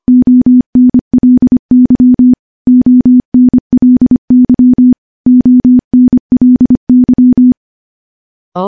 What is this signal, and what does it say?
Morse code, heard as a keyed tone, about 25 wpm, 250 Hz. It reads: ONLY ONLY ONLY
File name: morse / morse-training-25WPM